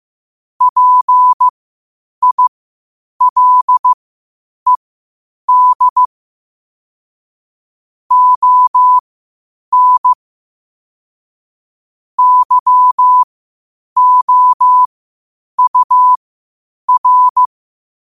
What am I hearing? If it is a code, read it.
Morse code, heard as a keyed tone, about 15 wpm, spaced as for 8 wpm, 1000 Hz. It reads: PILED ON YOUR
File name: qotdfiles / 8wpm